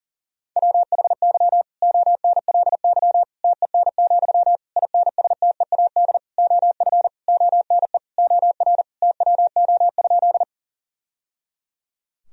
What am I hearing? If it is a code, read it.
Morse code, heard as a keyed tone, about 37 wpm, 700 Hz. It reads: WHY ONLY TEN, INSTEAD OF ONE OR TWO?
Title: Sentences from Top 1000 Words - Review Part 4 - 40wpm